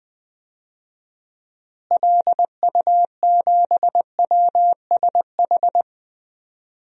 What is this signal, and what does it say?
Morse code, heard as a keyed tone, about 20 wpm, 700 Hz. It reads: LU7WSH
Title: DXCC-100 Call Signs - 6 Characters - Repeated 3x - 20wpm